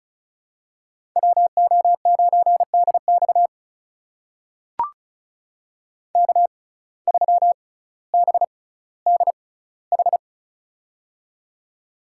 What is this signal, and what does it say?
Morse code, heard as a keyed tone, about 34 wpm, spaced as for 18 wpm, 700 Hz. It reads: WO9DX K3BDH
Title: USA Call Signs - 5 Characters - 6x Character Spacing - 35wpm